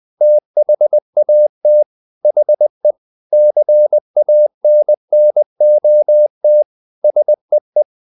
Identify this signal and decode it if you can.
Morse code, heard as a keyed tone, about 20 wpm, 600 Hz. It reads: THAT HE CANNOT SEE
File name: the old man and the sea / the old man and the sea part 1 20wpm